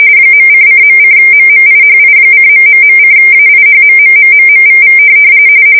RTTY, about 45.45 baud. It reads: .::.:MMIHI:...:AIHMMMMMVV